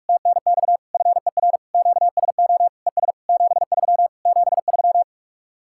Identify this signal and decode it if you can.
Morse code, heard as a keyed tone, about 59 wpm, spaced as for 52 wpm, 700 Hz. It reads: TNX FER QSO ES 73 73
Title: Speed-Racing Common QSO Phrases - 45wpm